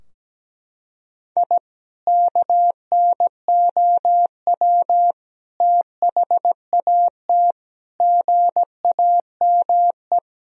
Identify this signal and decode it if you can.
Morse code, heard as a keyed tone, about 17 wpm, 700 Hz. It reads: I KNOW THAT GAME
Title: Sentences from Top 1000 Words - Review Part 3 - 17wpm